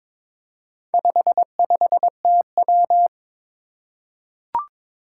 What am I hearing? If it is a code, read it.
Morse code, heard as a keyed tone, about 22 wpm, 700 Hz. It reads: 55TW
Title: Four Challenging Letter-Number Combinations 22wpm